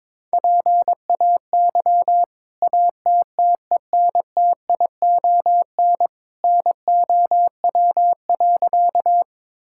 Morse code, heard as a keyed tone, about 22 wpm, 700 Hz. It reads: PAY ATTENTION NOW.
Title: Sentences from Top 1000 Words - Review Part 6 - 22wpm